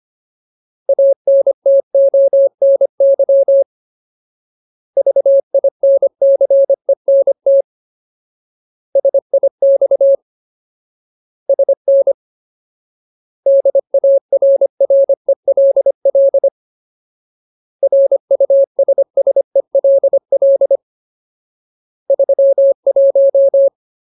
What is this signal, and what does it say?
Morse code, heard as a keyed tone, about 25 wpm, 550 Hz. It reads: ANTONY VINCENT SIX SN DARRELL RUSSELL 31
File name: Ultimate Word List / 25WPM Ultimate